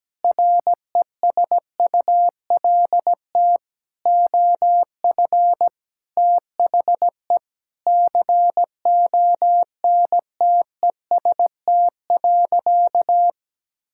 Morse code, heard as a keyed tone, about 17 wpm, 700 Hz. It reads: RESULT OF THE CONTEST.
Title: Sentences from Top 1600 Words 17wpm